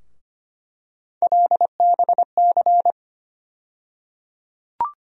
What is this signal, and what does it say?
Morse code, heard as a keyed tone, about 25 wpm, 700 Hz. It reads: LBC
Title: USA Call Sign Postfixes - 3 Letters 25wpm